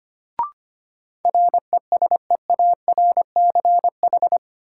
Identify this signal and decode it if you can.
Morse code, heard as a keyed tone, about 25 wpm, 700 Hz. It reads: RESEARCH
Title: Sentences from Top 1000 Words - Review Part 4 - New Word Drill - 25wpm